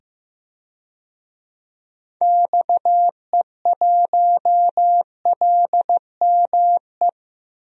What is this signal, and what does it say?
Morse code, heard as a keyed tone, about 15 wpm, 700 Hz. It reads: XE1LME
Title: DXCC-100 Call Signs - 6 Characters - 15wpm